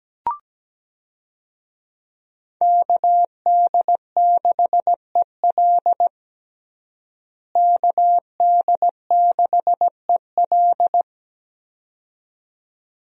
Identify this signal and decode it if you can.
Morse code, heard as a keyed tone, about 17 wpm, 700 Hz. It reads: KD6EL KD6EL
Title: USA Call Signs - 5 Characters - Repeated 2x - 17wpm